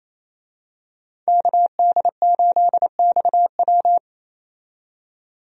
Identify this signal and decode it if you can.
Morse code, heard as a keyed tone, about 28 wpm, 700 Hz. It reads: KD8XW
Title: USA Call Signs - 5 Characters - 28wpm